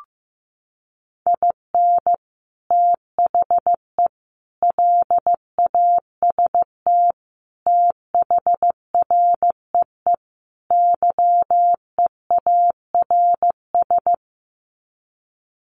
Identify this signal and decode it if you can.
Morse code, heard as a keyed tone, about 15 wpm, 700 Hz. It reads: IN THE LAST THREE YEARS
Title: Sets of 5 Words Encore2 15wpm